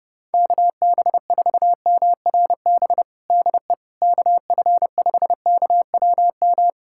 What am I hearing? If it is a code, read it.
Morse code, heard as a keyed tone, about 30 wpm, 700 Hz. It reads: KB4MRB DE KF5KWM
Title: USA Callsign DE Callsign 30wpm